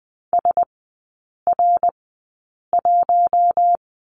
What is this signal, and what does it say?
Morse code, heard as a keyed tone, about 20 wpm, spaced as for 8 wpm, 700 Hz. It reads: SR1
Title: 053 Lesson 10e - Letter r - Three Characters 20wpm